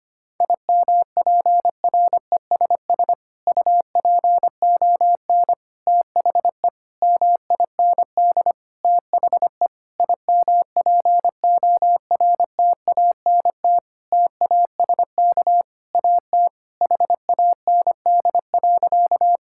Morse code, heard as a keyed tone, about 25 wpm, 700 Hz. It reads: IMPRESS UPON THE MIND THE IMPORTANT TASK AT HAND.
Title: Sentences from Top 1600 Words 25wpm